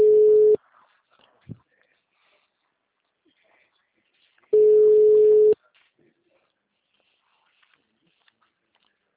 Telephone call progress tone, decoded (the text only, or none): ringback tone